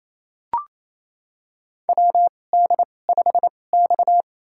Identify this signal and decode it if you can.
Morse code, heard as a keyed tone, about 28 wpm, spaced as for 20 wpm, 700 Hz. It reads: WD5X
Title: USA Call Signs - 4 Characters - 2x Character Spacing - 28wpm